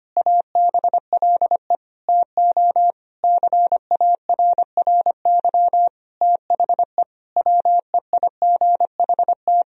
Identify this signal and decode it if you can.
Morse code, heard as a keyed tone, about 25 wpm, 700 Hz. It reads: ABLE TO CARRY THE WEIGHT
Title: Sentences from Top 1000 Words - Review Part 3 - 25wpm